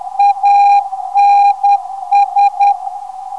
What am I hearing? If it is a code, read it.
Morse code, heard as a keyed tone, about 10 wpm, 810 Hz, with radio noise. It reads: ANS
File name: ndb zvok / ANS 450kHz